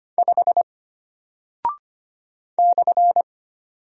Morse code, heard as a keyed tone, about 25 wpm, 700 Hz. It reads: 5 /
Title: Single Character (Letter:Number:Punctuation:Prosign) Alt 25wpm